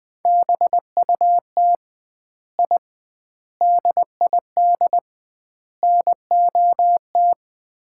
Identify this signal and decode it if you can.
Morse code, heard as a keyed tone, about 20 wpm, 700 Hz. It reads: BUT I DID NOT
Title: 2x Word-Spacing - Sets of 4 Words - 20wpm